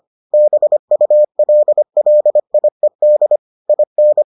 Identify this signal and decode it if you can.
Morse code, heard as a keyed tone, about 25 wpm, 600 Hz. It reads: BULLIED IN